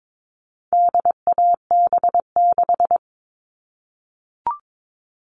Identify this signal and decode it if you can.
Morse code, heard as a keyed tone, about 22 wpm, 700 Hz. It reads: DAB6
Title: All Cut Numbers - 4-Digit 22wpm